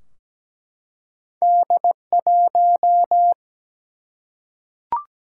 Morse code, heard as a keyed tone, about 17 wpm, 700 Hz. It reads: D1
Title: Single-Letter Single-Number 17wpm